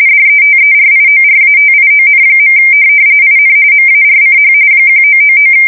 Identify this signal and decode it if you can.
RTTY, about 45.45 baud. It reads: DETERIORATION. IN MARCH 1999